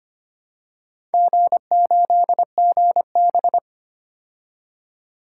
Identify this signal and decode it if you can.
Morse code, heard as a keyed tone, about 25 wpm, 700 Hz. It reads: G8GB
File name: Four Letter-Number Combinations - 25wpm